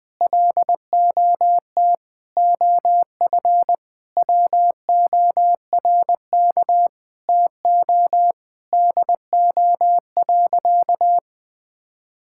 Morse code, heard as a keyed tone, about 20 wpm, 700 Hz. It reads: LOT OF WORK TO DO.
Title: Sentences with 9-Word Phrases - 20wpm